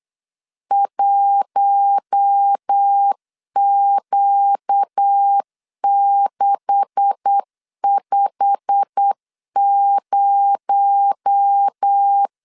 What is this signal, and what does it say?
Morse code, heard as a keyed tone, about 8 wpm, 800 Hz. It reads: 1Q650
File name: LEZIONI / 10 SIGNS / 10B INT 8wpm 3534